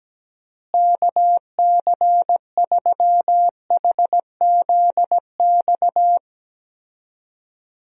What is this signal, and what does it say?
Morse code, heard as a keyed tone, about 17 wpm, 700 Hz. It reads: KC3HZX
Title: USA Call Signs - 6 Characters - Repeated 3x - 17wpm